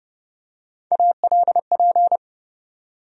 Morse code, heard as a keyed tone, about 30 wpm, 700 Hz. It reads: ALP
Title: USA Call Sign Postfixes - 3 Letters 30wpm